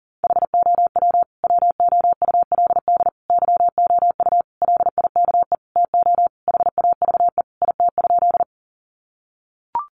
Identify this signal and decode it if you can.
Morse code, heard as a keyed tone, about 40 wpm, 700 Hz. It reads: HOW WOULD YOU LIKE TO HAVE IT?
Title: Sentences from Top 100 Words 40wpm